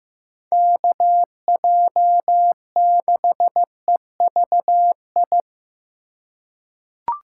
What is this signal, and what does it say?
Morse code, heard as a keyed tone, about 15 wpm, 700 Hz. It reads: KJ6EVI
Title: USA Call Signs - 6 Characters - 3x Character Spacing - 15wpm